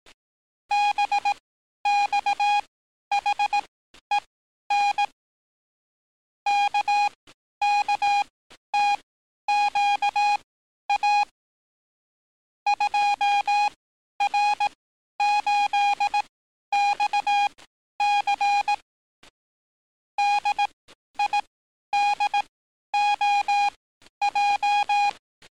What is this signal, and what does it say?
Morse code, heard as a keyed tone, about 18 wpm, spaced as for 11 wpm, 820 Hz. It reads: BXHEN KKTQA 2R8XC DIDOJ
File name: training / NAVPERS 11049to11060 / NAVPERS 11050 RA RB